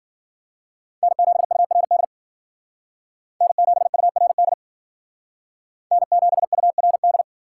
Morse code, heard as a keyed tone, about 50 wpm, 700 Hz. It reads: N7URD N7URD N7URD